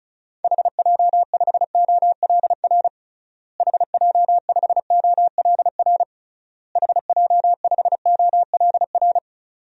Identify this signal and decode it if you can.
Morse code, heard as a keyed tone, about 36 wpm, 700 Hz. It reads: HJ5OLR HJ5OLR HJ5OLR
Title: DXCC-100 Call Signs - 6 Characters - Repeated 3x - 35wpm